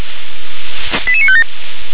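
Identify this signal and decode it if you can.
Five-tone selective call (ZVEI), heard as quick runs of five tones, tones of about 70 ms. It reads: 88957
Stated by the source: ZVEI five-tone call 88634 received by SDR.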